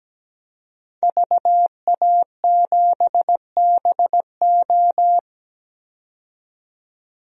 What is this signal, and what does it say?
Morse code, heard as a keyed tone, about 17 wpm, 700 Hz. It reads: VA7BO
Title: DXCC-100 Call Signs - 5 Characters - 17wpm